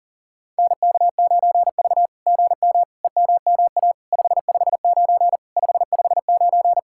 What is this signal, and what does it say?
Morse code, heard as a keyed tone, about 40 wpm, 700 Hz. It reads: NK9V GM EMMA 559 559
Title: SOTA QSO Practice 40wpm